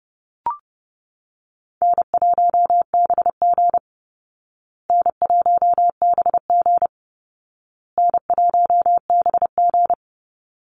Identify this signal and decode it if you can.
Morse code, heard as a keyed tone, about 30 wpm, 700 Hz. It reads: N1BG N1BG N1BG
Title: Call Signs Repeated 2x - 30wpm